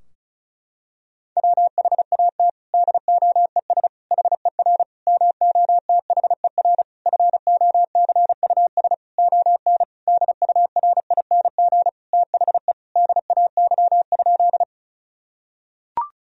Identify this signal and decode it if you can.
Morse code, heard as a keyed tone, about 34 wpm, 700 Hz. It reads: WHAT DOES HER MOTHER FOCUS ON DURING THE DAY?
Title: Sentences from Top 1000 Words - Review Part 4 - 35wpm